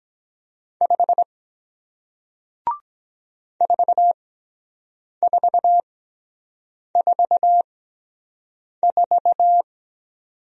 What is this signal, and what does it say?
Morse code, heard as a keyed tone, about 23 wpm, spaced as for 7 wpm, 700 Hz. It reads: 5 4444